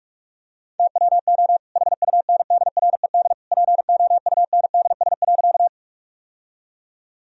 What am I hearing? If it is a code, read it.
Morse code, heard as a keyed tone, about 46 wpm, 700 Hz. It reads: TWO HUNDRED POUNDS.
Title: Sentences from Top 1000 Words - Review Part 2 - 45wpm